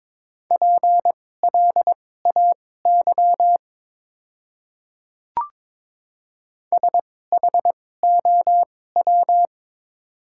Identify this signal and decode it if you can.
Morse code, heard as a keyed tone, about 22 wpm, spaced as for 16 wpm, 700 Hz. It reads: PLAY SHOW
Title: 2x Character Spacing - Top 200 Words - 22wpm